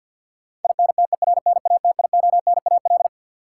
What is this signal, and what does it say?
Morse code, heard as a keyed tone, about 45 wpm, 700 Hz. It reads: INTERNATIONAL
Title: Sentences from Top 1000 Words - Review Part 4 - New Word Drill - 50wpm